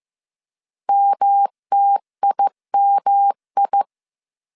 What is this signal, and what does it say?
Morse code, heard as a keyed tone, about 15 wpm, 800 Hz. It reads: MTIMI
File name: CW / DET / 1 3 1500C S A 14wpm 2447